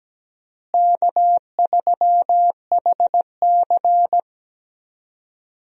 Morse code, heard as a keyed tone, about 17 wpm, 700 Hz. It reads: K3HC
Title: USA Call Signs - 4 Characters - Repeated 3x - 17wpm